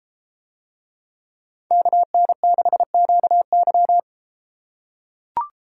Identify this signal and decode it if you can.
Morse code, heard as a keyed tone, about 32 wpm, 700 Hz. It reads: KN6QY